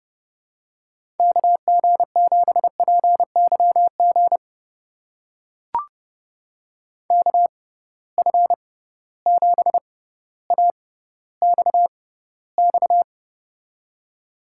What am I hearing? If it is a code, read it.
Morse code, heard as a keyed tone, about 30 wpm, spaced as for 15 wpm, 700 Hz. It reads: KG7PYG KF7AXX
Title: USA Call Signs - 6 Characters - 6x Character Spacing - 30wpm